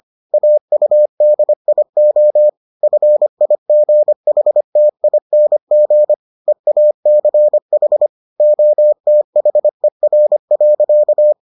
Morse code, heard as a keyed tone, about 25 wpm, 600 Hz. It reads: AUDIO FIGHTING EACH OTHER.